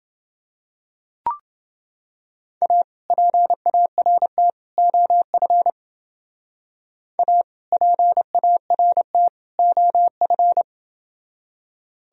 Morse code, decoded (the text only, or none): A PART OF A PART OF